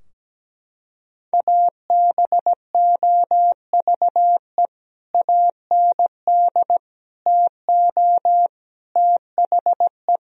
Morse code, heard as a keyed tone, about 17 wpm, 700 Hz. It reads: ABOVE AND TO THE
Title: Sentences from Top 1000 Words - Review Part 2 - 17wpm